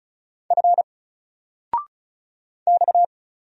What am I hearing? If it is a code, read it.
Morse code, heard as a keyed tone, about 36 wpm, 700 Hz. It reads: F X
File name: Single-Letter-Number-35wpm